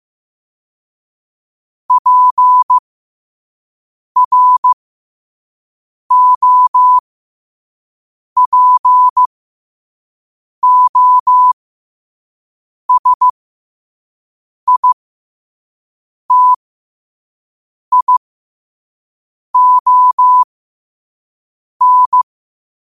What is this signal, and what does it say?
Morse code, heard as a keyed tone, about 15 wpm, spaced as for 5 wpm, 1000 Hz. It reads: PROPOSITION
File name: qotdfiles / 5wpm